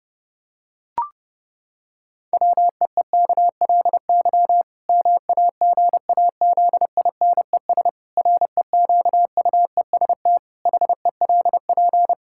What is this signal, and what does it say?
Morse code, heard as a keyed tone, about 30 wpm, 700 Hz. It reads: WEEKLY MAGAZINES REQUEST HELP
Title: Pangram Challenge v2 - 30wpm